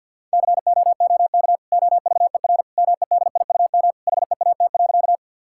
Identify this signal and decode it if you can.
Morse code, heard as a keyed tone, about 50 wpm, 700 Hz. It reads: COOK OVER MEDIUM HEAT.
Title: Sentences from Top 1000 Words - Review Part 6 - 50wpm